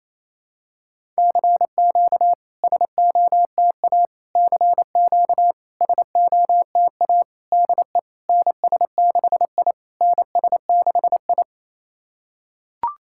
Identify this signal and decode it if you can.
Morse code, heard as a keyed tone, about 28 wpm, 700 Hz. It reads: CQ SOTA CQ SOTA DE NS6I NS6I
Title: SOTA QSO Practice 28wpm